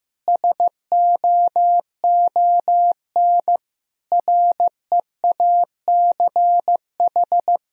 Morse code, heard as a keyed tone, about 15 wpm, 700 Hz. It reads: SOON REACH
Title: Sentences from Top 400 Words and Abbreviations 15wpm